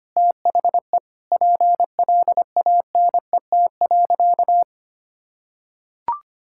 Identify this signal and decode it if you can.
Morse code, heard as a keyed tone, about 25 wpm, 700 Hz. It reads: THE PLANET.
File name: practice / 25wpm / Difficult Sentences with Hints - 25wpm